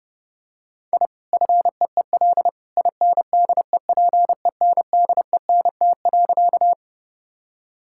Morse code, heard as a keyed tone, about 30 wpm, 700 Hz. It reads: I FEEL INDEPENDENT.
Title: Sentences from Top 1000 Words - Review Part 6 - 30wpm